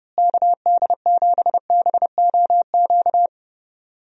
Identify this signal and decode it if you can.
Morse code, heard as a keyed tone, about 30 wpm, 700 Hz. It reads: KD7BOQ
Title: USA Call Signs - 6 Characters - Repeated 3x - 30wpm